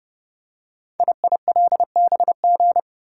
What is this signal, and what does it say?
Morse code, heard as a keyed tone, about 29 wpm, 690 Hz. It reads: IILBG
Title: Code Groups - 5 Letters - 30wpm